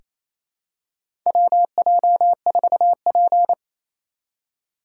Morse code, heard as a keyed tone, about 28 wpm, 700 Hz. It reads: WJ4P
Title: USA Call Signs - 4 Characters - 6x Character Spacing - 28wpm